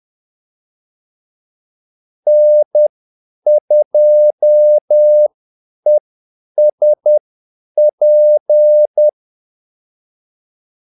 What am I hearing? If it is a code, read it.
Morse code, heard as a keyed tone, about 10 wpm, spaced as for 8 wpm, 600 Hz. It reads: N2ESP
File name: Callsigns-10-8wpm